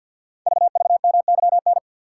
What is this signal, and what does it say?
Morse code, heard as a keyed tone, about 52 wpm, 700 Hz. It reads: VVMYN
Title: Code Groups - 5 Letters - 50wpm